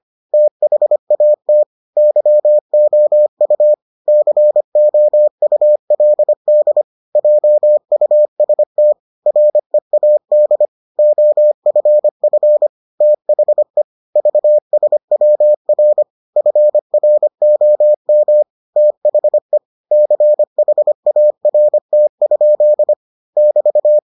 Morse code, heard as a keyed tone, about 25 wpm, 600 Hz. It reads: THAT YOU COULD JUST READ OFF THE VSWR FROM THE CHART? <BT>